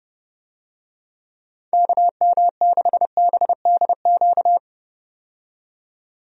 Morse code, heard as a keyed tone, about 32 wpm, 700 Hz. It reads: KM6BDQ
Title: USA Call Signs - 6 Characters - 30wpm